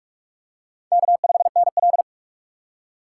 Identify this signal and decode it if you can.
Morse code, heard as a keyed tone, about 44 wpm, 700 Hz. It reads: K5NL